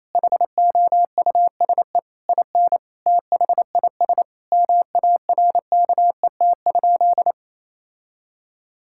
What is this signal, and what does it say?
Morse code, heard as a keyed tone, about 28 wpm, 700 Hz. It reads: HOUSE IN THIS MARKET?
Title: Sentences from Top 1000 Words - Review Part 5 - 28wpm